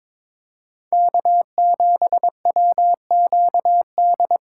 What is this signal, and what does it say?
Morse code, heard as a keyed tone, about 22 wpm, 700 Hz. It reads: K7WQD
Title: USA Call Signs - 5 Characters - 2x Character Spacing - 22wpm